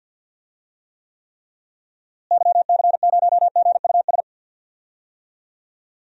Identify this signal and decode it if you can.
Morse code, heard as a keyed tone, about 52 wpm, 700 Hz. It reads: YC0GUS
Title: DXCC-100 Call Signs - 6 Characters - Repeated 3x - 50wpm